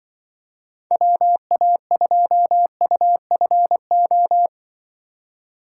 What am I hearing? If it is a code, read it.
Morse code, heard as a keyed tone, about 24 wpm, 700 Hz. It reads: WA2UFO
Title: Speed Racing - USA 2x3 Call Signs 20wpm